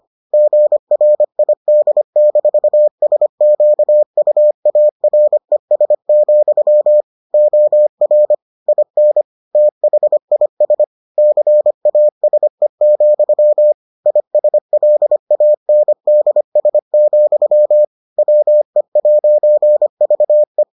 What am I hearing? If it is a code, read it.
Morse code, heard as a keyed tone, about 25 wpm, 600 Hz. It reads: GRID-SQUARES, OR IN THIS CASE, ISLANDS, WE'VE